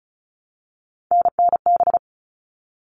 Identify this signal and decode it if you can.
Morse code, heard as a keyed tone, about 34 wpm, 700 Hz. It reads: NNB
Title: All Cut Numbers - 3-Digit 35wpm